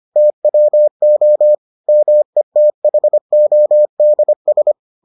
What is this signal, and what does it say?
Morse code, heard as a keyed tone, about 25 wpm, 600 Hz. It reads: TWO METHODS